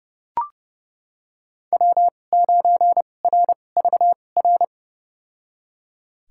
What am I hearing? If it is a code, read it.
Morse code, heard as a keyed tone, about 30 wpm, spaced as for 22 wpm, 700 Hz. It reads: W9RVR
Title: USA Call Signs - 5 Characters - 2x Character Spacing - 30wpm